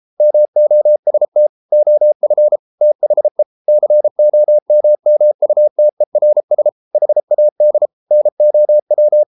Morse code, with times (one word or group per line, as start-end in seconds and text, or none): 0.2-1.5 MOST
1.7-2.6 OF
2.8-3.4 THE
3.7-6.7 COMMUTERS
7.0-7.9 HAD
8.1-9.2 NOW